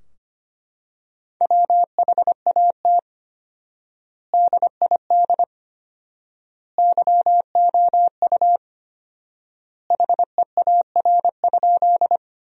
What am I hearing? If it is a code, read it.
Morse code, heard as a keyed tone, about 25 wpm, 700 Hz. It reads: WHAT DID YOU HEAR?
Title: 4x Word-Spacing - Sentences from Top 300 Words - 25wpm